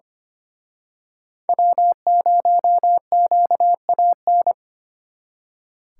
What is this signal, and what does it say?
Morse code, heard as a keyed tone, about 29 wpm, 700 Hz. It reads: W0QAN
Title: Speed Racing - USA 1x3 Call Signs 25wpm